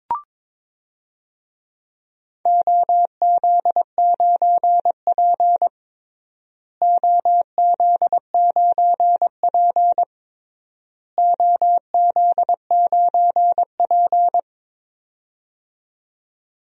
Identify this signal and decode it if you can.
Morse code, heard as a keyed tone, about 21 wpm, 700 Hz. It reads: OZ9P OZ9P OZ9P
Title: DXCC-100 Call Signs - 4 Characters - Repeated 3x - 22wpm